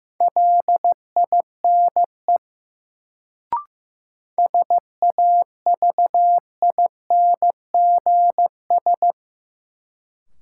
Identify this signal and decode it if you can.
Morse code, heard as a keyed tone, about 15 wpm, 700 Hz. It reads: LINE SAVINGS
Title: Sentences from Top 1200 Words - New Word Drill Alt - 15wpm